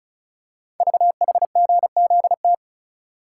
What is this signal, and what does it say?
Morse code, heard as a keyed tone, about 36 wpm, 690 Hz. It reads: VHGZT
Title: Code Groups - 5 Letters - 35wpm